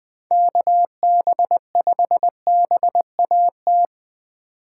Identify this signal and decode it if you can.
Morse code, heard as a keyed tone, about 24 wpm, 700 Hz. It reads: KB5BAT
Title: Speed Racing - USA Call Signs 20wpm